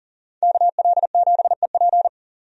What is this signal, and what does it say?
Morse code, heard as a keyed tone, about 40 wpm, 700 Hz. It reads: KL7EP
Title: DXCC-100 Call Signs - 5 Characters - 40wpm